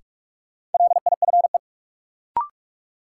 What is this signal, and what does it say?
Morse code, heard as a keyed tone, about 42 wpm, 700 Hz. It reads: LIFE
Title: Sets of 5 Words - New Word Drill - 45wpm